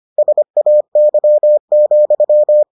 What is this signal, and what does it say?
Morse code, heard as a keyed tone, about 25 wpm, 600 Hz. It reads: SAY,